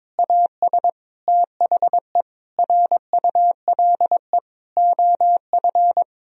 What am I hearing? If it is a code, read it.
Morse code, heard as a keyed tone, about 22 wpm, 700 Hz. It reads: AS THE RULE OF
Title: Sentences from Top 1000 Words - Review Part 2 - 22wpm